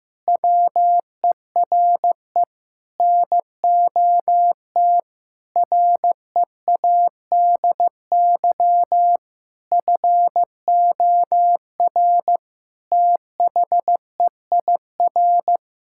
Morse code, heard as a keyed tone, about 15 wpm, 700 Hz. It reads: WERE NOT READY FOR THEIR
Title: Sentences from Top 1000 Words - Review Part 6 - 15wpm